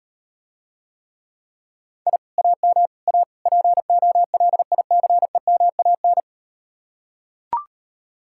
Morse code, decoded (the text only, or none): I AM A POLICEMAN